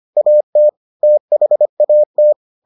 Morse code, heard as a keyed tone, about 25 wpm, 600 Hz. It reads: AT THAT